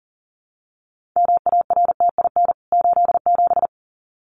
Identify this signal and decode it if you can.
Morse code, heard as a keyed tone, about 40 wpm, 700 Hz. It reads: MARTIN 87
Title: CWT Exchange 40wpm